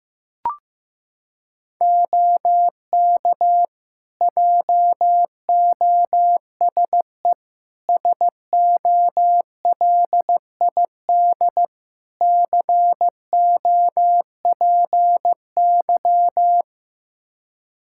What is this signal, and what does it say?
Morse code, heard as a keyed tone, about 15 wpm, 700 Hz. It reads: OK JOSE SOLID COPY
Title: K1USN SST QSO Practice 15wpm